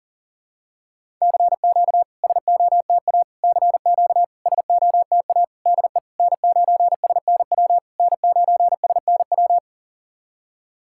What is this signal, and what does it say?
Morse code, heard as a keyed tone, about 40 wpm, 690 Hz. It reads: CQ SOTA CQ SOTA DE N9SNW N9SNW
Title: SOTA QSO Practice 40wpm